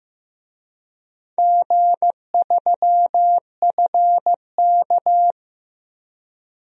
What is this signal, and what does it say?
Morse code, heard as a keyed tone, about 16 wpm, 700 Hz. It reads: G3FK
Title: DXCC-100 Call Signs - 4 Characters - Repeated 2x - 15wpm